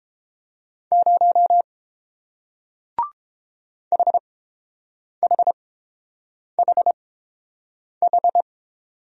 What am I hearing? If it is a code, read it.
Morse code, heard as a keyed tone, about 30 wpm, spaced as for 7 wpm, 700 Hz. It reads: 0 HHHH